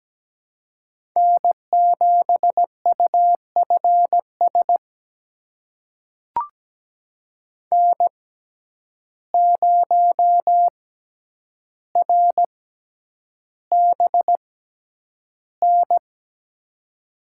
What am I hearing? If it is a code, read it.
Morse code, heard as a keyed tone, about 17 wpm, spaced as for 9 wpm, 700 Hz. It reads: N7UFS N0RBN
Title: USA Call Signs - 5 Characters - 6x Character Spacing - 17wpm